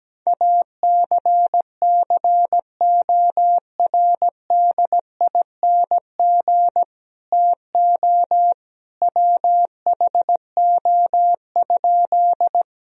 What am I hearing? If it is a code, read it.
Morse code, heard as a keyed tone, about 17 wpm, 700 Hz. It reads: ACCORDING TO WHO?
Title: Sentences from Top 1000 Words - Review Part 4 - 17wpm